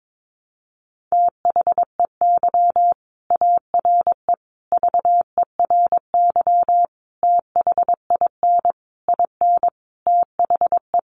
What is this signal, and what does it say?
Morse code, heard as a keyed tone, about 22 wpm, 700 Hz. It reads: THEY ARE VERY THIN IN THE
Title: Sentences from Top 1200 Words 22wpm